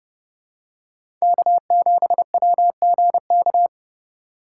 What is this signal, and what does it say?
Morse code, heard as a keyed tone, about 30 wpm, 700 Hz. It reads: K7WGK
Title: USA Call Signs - 5 Characters - 4x Character Spacing - 30wpm